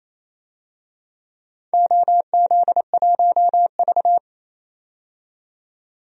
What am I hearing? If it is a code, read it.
Morse code, heard as a keyed tone, about 28 wpm, 700 Hz. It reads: OZ1V
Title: DXCC-100 Call Signs - 4 Characters - 28wpm